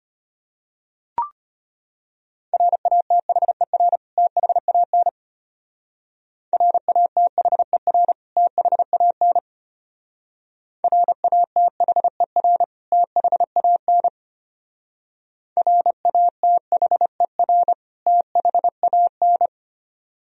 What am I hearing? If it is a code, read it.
Morse code, heard as a keyed tone, about 31 wpm, 700 Hz. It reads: RATHER THAN RATHER THAN RATHER THAN RATHER THAN
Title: Speed-Racing - Sets of 2 Words 25wpm